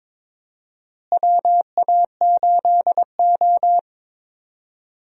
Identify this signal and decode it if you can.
Morse code, heard as a keyed tone, about 22 wpm, 700 Hz. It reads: WA8O
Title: USA Call Signs - 4 Characters - Repeated 3x - 22wpm